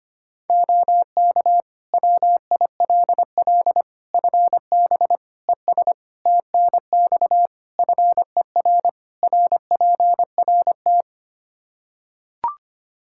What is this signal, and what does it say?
Morse code, heard as a keyed tone, about 25 wpm, 700 Hz. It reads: OK WILL FB ES TNX FER RPRT
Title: Practice QSO - Exchange 2 - Info - 25wpm